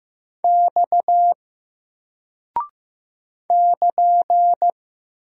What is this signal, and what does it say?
Morse code, heard as a keyed tone, about 15 wpm, 700 Hz. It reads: X <KN>
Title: Single Character (Letter:Number:Punctuation:Prosign) Alt 15wpm